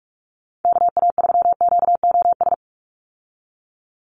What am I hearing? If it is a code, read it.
Morse code, heard as a keyed tone, about 46 wpm, 690 Hz. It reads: KA3QOS
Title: USA Call Signs 45wpm